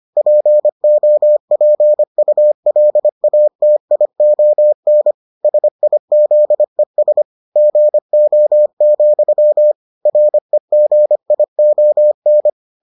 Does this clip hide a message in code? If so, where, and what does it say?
Yes: Morse code from 0.2 to 12.5 s, heard as a keyed tone, about 25 wpm, 600 Hz. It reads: POPULATION SIZES GO, REGION